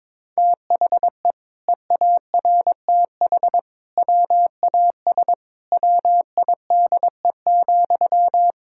Morse code, heard as a keyed tone, about 22 wpm, 700 Hz. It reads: THE EARTH WAS WIDE,
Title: Sentences from Top 1000 Words - Review Part 6 - 22wpm